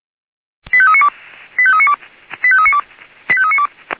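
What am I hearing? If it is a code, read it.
Five-tone selective call (ZVEI), heard as quick runs of five tones, tones of about 70 ms. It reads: 85392 85392 85392 85392
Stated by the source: ZVEI five-tone selective call received with an SDR.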